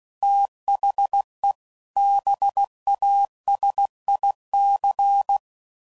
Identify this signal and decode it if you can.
Morse code, heard as a keyed tone, about 16 wpm, 800 Hz. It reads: THE BASIC
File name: CW Code Groups @15WPM/Technical Subjects